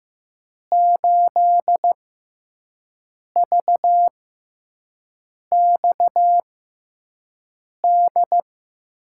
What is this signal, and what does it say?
Morse code, heard as a keyed tone, about 15 wpm, spaced as for 5 wpm, 700 Hz. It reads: 8VXD
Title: USA Call Signs - 5 Characters - 6x Character Spacing - 15wpm